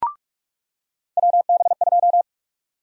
Morse code, heard as a keyed tone, about 46 wpm, 700 Hz. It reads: WB2
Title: USA Call Sign Prefixes - 45wpm